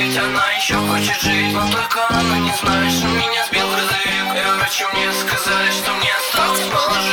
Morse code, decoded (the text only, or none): none